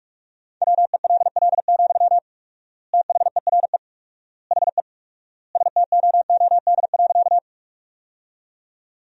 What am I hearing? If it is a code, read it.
Morse code, heard as a keyed tone, about 46 wpm, 700 Hz. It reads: WELL, THERE HE STOOD.
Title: 4x Word-Spacing - Sentences from Top 500 Words - 45wpm